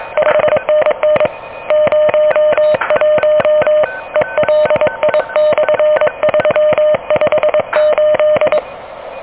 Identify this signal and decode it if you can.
Morse code, heard as a keyed tone, about 22 wpm, 600 Hz, with noise. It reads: 5NN 01 ELI/358
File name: OK QRP 2023 / OK QRP 2023